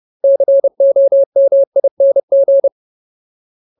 Morse code, heard as a keyed tone, about 30 wpm, 550 Hz. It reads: COMING
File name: Ultimate Word List / 30WPM Ultimate